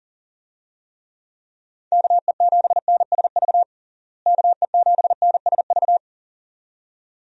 Morse code, heard as a keyed tone, about 40 wpm, 700 Hz. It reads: KE7NSV KE7NSV
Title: USA Call Signs - 6 Characters - Repeated 2x - 40wpm